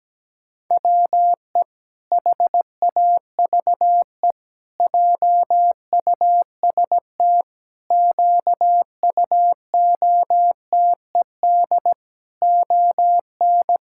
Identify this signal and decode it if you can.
Morse code, heard as a keyed tone, about 17 wpm, 700 Hz. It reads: WE HAVE JUST QUOTED ON
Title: Pangram Challenge v2 - 17wpm